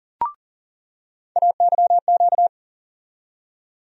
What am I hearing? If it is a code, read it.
Morse code, heard as a keyed tone, about 40 wpm, 700 Hz. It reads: AYQ
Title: USA Call Sign Postfixes - 3 Letters 40wpm